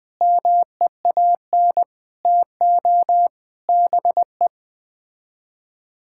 Morse code, heard as a keyed tone, about 20 wpm, 700 Hz. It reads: MEAN TO BE
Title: Sets of 6 Words - 20wpm